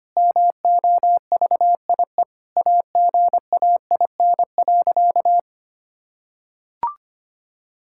Morse code, heard as a keyed tone, about 25 wpm, 700 Hz. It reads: MOVIE AGAIN.
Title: Sentences from Top 1000 Words - Review Part 5 - 25wpm